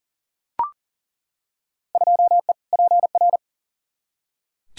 Morse code, heard as a keyed tone, about 40 wpm, 700 Hz. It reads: 2E PR
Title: Field Day Exchanges Encore - 40wpm